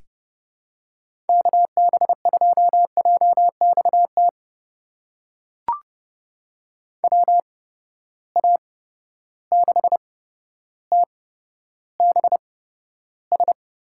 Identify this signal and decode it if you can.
Morse code, heard as a keyed tone, about 30 wpm, spaced as for 13 wpm, 700 Hz. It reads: KB2JXT WA6TBS
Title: USA Call Signs - 6 Characters - 8x Character Spacing - 30wpm